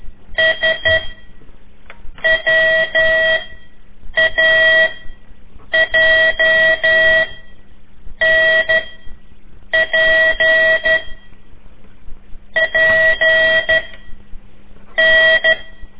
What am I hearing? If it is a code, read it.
Morse code, heard as a keyed tone, about 12 wpm, 1950 Hz. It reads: S W A J N P P N
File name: ik1weg2 / CORSI CW / lez4-3